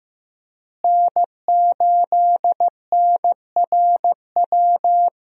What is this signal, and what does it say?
Morse code, heard as a keyed tone, about 15 wpm, 700 Hz. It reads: N8NRW
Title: USA Call Signs - 5 Characters - Repeated 2x - 15wpm